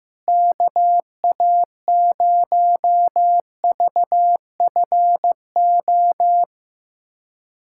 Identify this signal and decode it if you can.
Morse code, heard as a keyed tone, about 16 wpm, 700 Hz. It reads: KA0VFO
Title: USA Callsign DE Callsign - 6 Characters - 15wpm